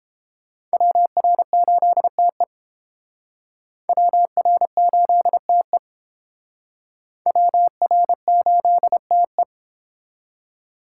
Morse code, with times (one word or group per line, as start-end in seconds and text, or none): 0.7-2.4 WR8TE
3.9-5.8 WR8TE
7.3-9.4 WR8TE